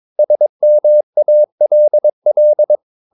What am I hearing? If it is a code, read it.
Morse code, heard as a keyed tone, about 22 wpm, 600 Hz. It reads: SMALL
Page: Romanen Never af Ken Follet på engelsk.